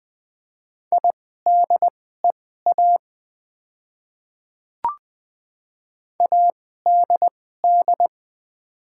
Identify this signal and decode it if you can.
Morse code, heard as a keyed tone, about 20 wpm, spaced as for 14 wpm, 700 Hz. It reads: IDEA ADD
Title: 2x Character Spacing - Top 300 Words - 20wpm